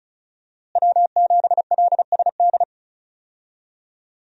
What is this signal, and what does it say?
Morse code, heard as a keyed tone, about 34 wpm, 700 Hz. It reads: W7LSD
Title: USA Call Signs - 5 Characters - Repeated 3x - 35wpm